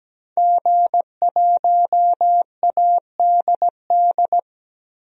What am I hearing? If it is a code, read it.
Morse code, heard as a keyed tone, about 17 wpm, 700 Hz. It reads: G1ADD
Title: DXCC-100 Call Signs - 5 Characters - 17wpm